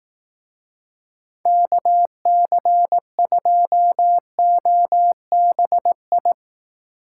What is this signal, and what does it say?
Morse code, heard as a keyed tone, about 18 wpm, 700 Hz. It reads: KC2OBI
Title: Speed Racing - USA 2x3 Call Signs 15wpm